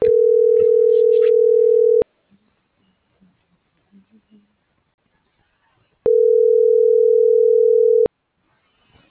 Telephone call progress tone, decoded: ringback tone

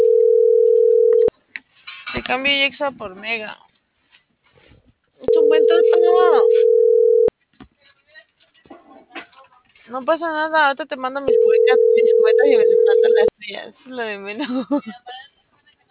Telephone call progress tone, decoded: ringback tone

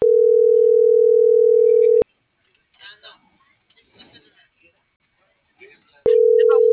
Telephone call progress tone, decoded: ringback tone

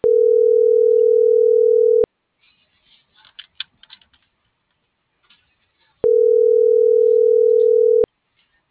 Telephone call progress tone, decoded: ringback tone

